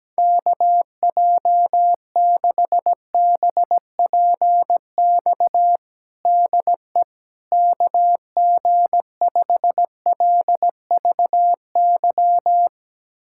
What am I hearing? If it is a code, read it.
Morse code, heard as a keyed tone, about 17 wpm, 700 Hz. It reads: KJ6BPX DE KG5LVY
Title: USA Callsign DE Callsign - 6 Characters - 17wpm